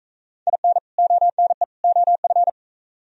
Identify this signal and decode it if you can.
Morse code, heard as a keyed tone, about 42 wpm, 700 Hz. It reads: IN ONE OF